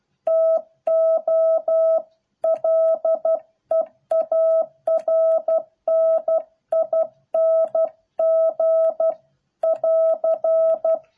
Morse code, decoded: TO LEARNING <AR>

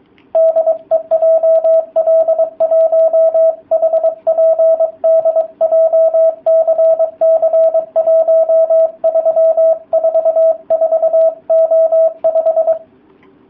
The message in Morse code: DE JL1HPD JCC1344O5

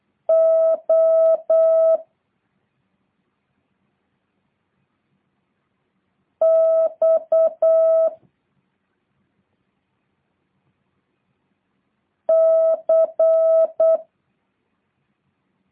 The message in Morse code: O X C

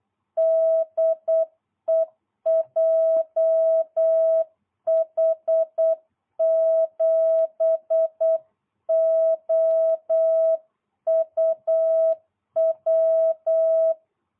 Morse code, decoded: DE JH7OUW

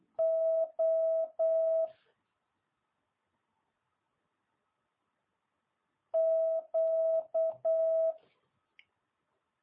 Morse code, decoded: O Q

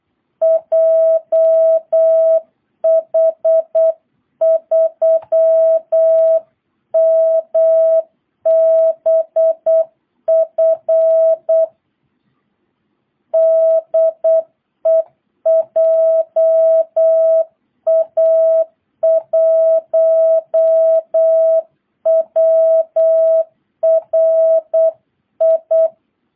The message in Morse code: JH3MBF DE JA1WRI

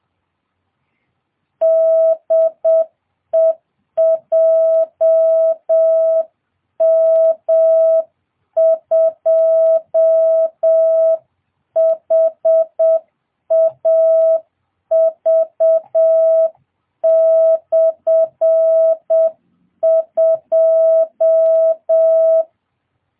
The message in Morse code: DE JM2HAV/2

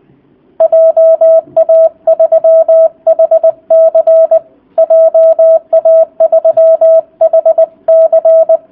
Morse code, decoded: JA3HC JA3HC

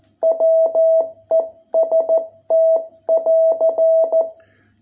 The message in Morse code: WEST <AR>

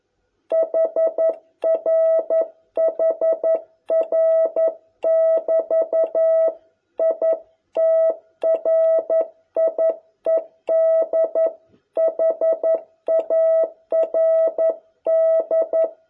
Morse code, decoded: HR HR <BT> I TRIED HARD

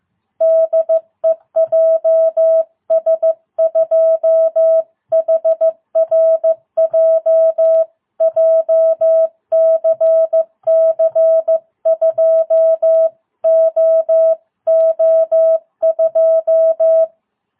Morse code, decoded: DE JS2HRJ JCC 2OO2

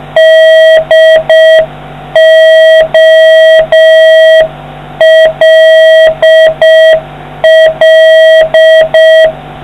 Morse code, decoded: DOLL

